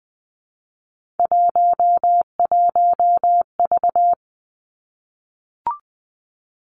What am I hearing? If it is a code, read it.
Morse code, heard as a keyed tone, about 20 wpm, 700 Hz. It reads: 11V
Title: Three Challenging Letter-Number Combinations 20wpm